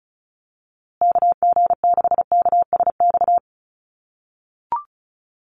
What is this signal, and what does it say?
Morse code, heard as a keyed tone, about 34 wpm, 700 Hz. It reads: KG6KSX
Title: USA Call Signs Repeated 3x 35wpm